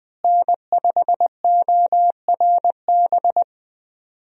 Morse code, heard as a keyed tone, about 20 wpm, 700 Hz. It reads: N5ORB
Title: USA Call Signs - 5 Characters - Repeated 2x - 20wpm